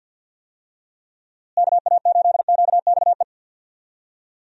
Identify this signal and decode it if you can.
Morse code, heard as a keyed tone, about 51 wpm, 700 Hz. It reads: KA8QXE
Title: USA Call Signs - 6 Characters - Repeated 3x - 50wpm